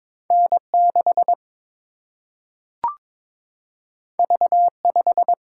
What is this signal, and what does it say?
Morse code, heard as a keyed tone, about 22 wpm, 700 Hz. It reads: N 6 V 5
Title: Single-Letter Single-Number 22wpm